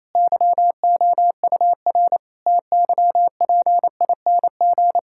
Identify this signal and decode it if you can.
Morse code, heard as a keyed tone, about 28 wpm, 700 Hz. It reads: YOUR TYPING
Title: Pangram Challenge v2 - 28wpm